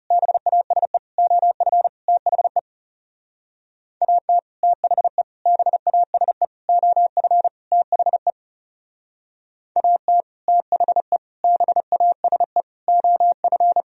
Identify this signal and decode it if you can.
Morse code, heard as a keyed tone, about 38 wpm, 700 Hz. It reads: BASE OF THE AT THE BASE OF THE AT THE BASE OF